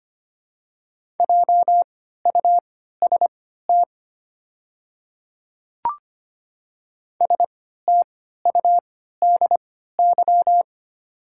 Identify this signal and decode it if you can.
Morse code, heard as a keyed tone, about 25 wpm, spaced as for 14 wpm, 700 Hz. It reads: JUST STUDY
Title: 3x Character Spacing - Top 300 Words - 25wpm